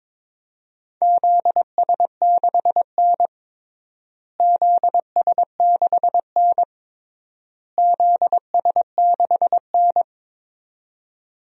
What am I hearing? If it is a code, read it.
Morse code, heard as a keyed tone, about 22 wpm, 700 Hz. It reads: ZS6N ZS6N ZS6N
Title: DXCC-100 Call Signs - 4 Characters - Repeated 3x - 22wpm